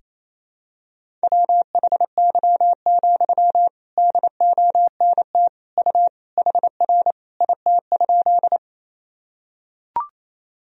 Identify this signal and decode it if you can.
Morse code, heard as a keyed tone, about 28 wpm, 700 Hz. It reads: WHY, DONT U HR IT?
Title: Sentences from Top 300 Words and Abbreviations 28wpm